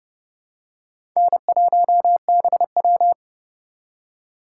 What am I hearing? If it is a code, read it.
Morse code, heard as a keyed tone, about 30 wpm, 700 Hz. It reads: N1BW
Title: USA Call Signs - 2x Character Spacing - 30wpm